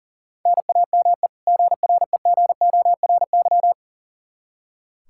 Morse code, heard as a keyed tone, about 37 wpm, 700 Hz. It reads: NAME GREGORY
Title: Practice QSO - Exchange 1 - Beginning - 40wpm